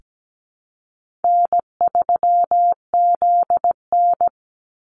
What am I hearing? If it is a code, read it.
Morse code, heard as a keyed tone, about 17 wpm, 700 Hz. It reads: N3ZN
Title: Easy Call Signs Repeated 2x - 17wpm